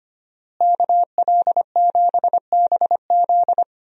Morse code, heard as a keyed tone, about 25 wpm, 700 Hz. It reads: KL7BZ
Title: DXCC-100 Call Signs - 5 Characters - Repeated 2x - 25wpm